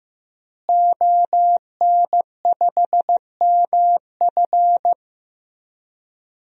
Morse code, heard as a keyed tone, about 15 wpm, 700 Hz. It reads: ON5MF
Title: DXCC-100 Call Signs - 5 Characters - Repeated 3x - 15wpm